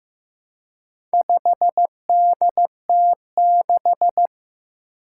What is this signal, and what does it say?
Morse code, heard as a keyed tone, about 15 wpm, 700 Hz. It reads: 5DT6
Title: Four Challenging Letter-Number Combinations 15wpm